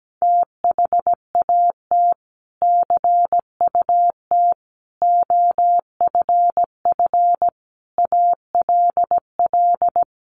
Morse code, heard as a keyed tone, about 17 wpm, 700 Hz. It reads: THAT CUT OFF ALL
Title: Sentences from Top 1100 Words 17wpm